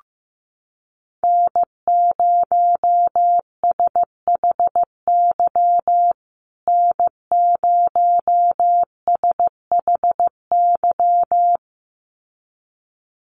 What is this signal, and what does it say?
Morse code, heard as a keyed tone, about 15 wpm, 700 Hz. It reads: N0SHY N0SHY
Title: USA Call Signs Repeated 2x 15wpm